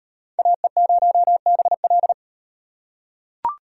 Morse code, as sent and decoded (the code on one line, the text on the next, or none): .- . ----- -... .-..
AE0BL